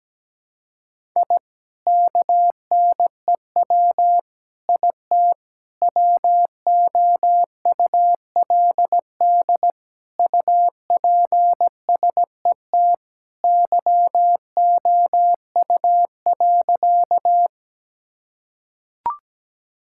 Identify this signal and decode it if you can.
Morse code, heard as a keyed tone, about 17 wpm, 700 Hz. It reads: I KNEW IT WOULD UPSET YOU.
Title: Sentences from Top 1000 Words - Review Part 5 - 17wpm